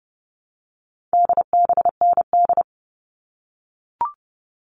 Morse code, heard as a keyed tone, about 30 wpm, 700 Hz. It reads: DBND
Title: All Cut Numbers - 4-Digit 30wpm